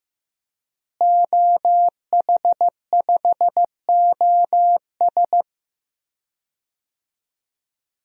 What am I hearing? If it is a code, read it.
Morse code, heard as a keyed tone, about 16 wpm, 700 Hz. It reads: OH5OS
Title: DXCC-100 Call Signs - 5 Characters - Repeated 2x - 15wpm